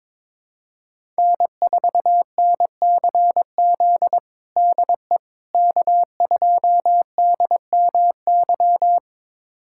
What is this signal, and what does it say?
Morse code, heard as a keyed tone, about 22 wpm, 700 Hz. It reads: N4NCZ DE K2DMY
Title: USA Callsign DE Callsign - 5 Characters - 22wpm